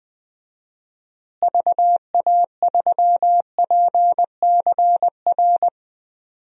Morse code, heard as a keyed tone, about 21 wpm, 700 Hz. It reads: VA3PCR
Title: DXCC-100 Call Signs - 6 Characters - Repeated 3x - 20wpm